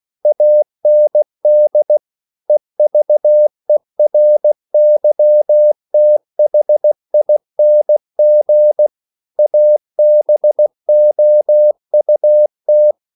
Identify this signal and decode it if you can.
Morse code, heard as a keyed tone, about 16 wpm, 600 Hz. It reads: AND EVERYTHING ABOUT
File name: CW / Practice / EnglishQuotations0116